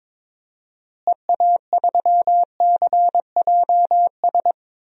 Morse code, heard as a keyed tone, about 22 wpm, 700 Hz. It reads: EA3CJS
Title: DXCC-100 Call Signs - 6 Characters - Repeated 2x - 22wpm